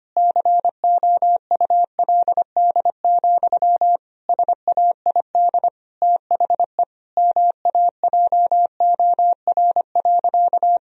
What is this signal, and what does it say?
Morse code, heard as a keyed tone, about 25 wpm, 700 Hz. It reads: COULD, SAID THE MAJOR.
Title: Sentences from Top 1000 Words - Review Part 4 - 25wpm